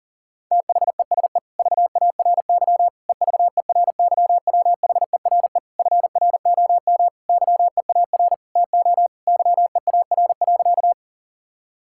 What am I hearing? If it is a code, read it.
Morse code, heard as a keyed tone, about 40 wpm, 700 Hz. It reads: THESE VARY EVERYWHERE FROM YEAR TO YEAR.
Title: Sentences from Top 1600 Words 40wpm